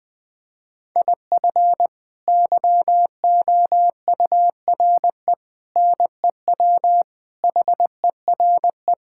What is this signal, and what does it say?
Morse code, heard as a keyed tone, about 20 wpm, 700 Hz. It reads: IF YOURE NEW HERE
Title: Sets of 8 Words - 20wpm